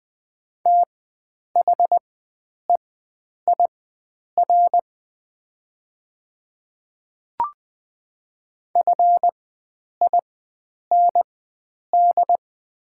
Morse code, heard as a keyed tone, about 20 wpm, spaced as for 9 wpm, 700 Hz. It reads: THEIR FIND
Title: 4x Character Spacing - Top 100 Words - 20wpm